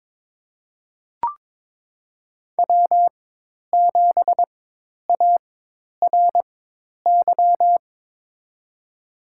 Morse code, heard as a keyed tone, about 22 wpm, spaced as for 10 wpm, 700 Hz. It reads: W7ARY
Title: USA Call Signs - 4x Character Spacing - 22wpm